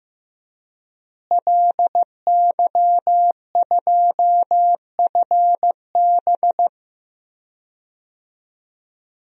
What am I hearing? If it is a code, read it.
Morse code, heard as a keyed tone, about 15 wpm, 700 Hz. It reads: LY2FB
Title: DXCC-100 Call Signs - 5 Characters - Repeated 2x - 15wpm